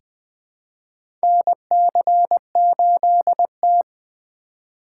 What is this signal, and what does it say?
Morse code, heard as a keyed tone, about 20 wpm, 700 Hz. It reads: NC8T
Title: USA Call Signs - 4x Character Spacing - 20wpm